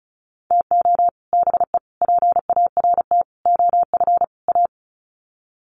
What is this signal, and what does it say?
Morse code, heard as a keyed tone, about 34 wpm, 700 Hz. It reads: TO BE PART OF A
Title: Sets of 5 Words Encore2 35wpm